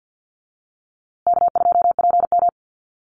Morse code, heard as a keyed tone, about 48 wpm, 700 Hz. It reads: K2PM
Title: USA Call Signs Repeated 3x 50wpm